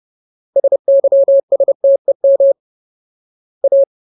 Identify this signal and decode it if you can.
Morse code, heard as a keyed tone, about 30 wpm, 550 Hz. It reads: SYSTEM A
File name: Ultimate Word List / 30WPM Ultimate